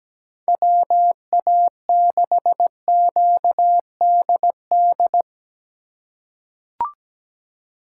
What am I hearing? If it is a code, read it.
Morse code, heard as a keyed tone, about 17 wpm, 700 Hz. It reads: WA6QDD
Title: USA Call Signs - 6 Characters - 4x Character Spacing - 17wpm